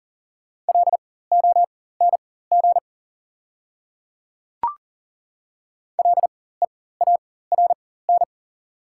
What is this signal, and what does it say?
Morse code, heard as a keyed tone, about 40 wpm, spaced as for 19 wpm, 700 Hz. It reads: LONG LEARN
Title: 4x Character Spacing - Top 300 Words - 40wpm